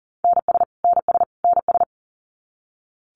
Morse code, heard as a keyed tone, about 40 wpm, 700 Hz. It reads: NS NS NS
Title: Canadian Provinces and Territories - Repeated 3x - 40wpm